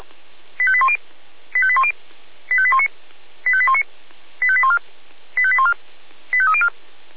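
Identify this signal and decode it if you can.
Five-tone selective call (ZVEI), heard as quick runs of five tones, tones of about 70 ms. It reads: 85810 85810 85819 85819 85814 85814 85304
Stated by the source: Five-tone ZVEI selective call received on an SDR.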